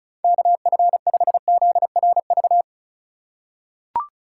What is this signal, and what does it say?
Morse code, heard as a keyed tone, about 36 wpm, 690 Hz. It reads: KF5ZRV
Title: USA Call Signs - 3x Character Spacing - 35wpm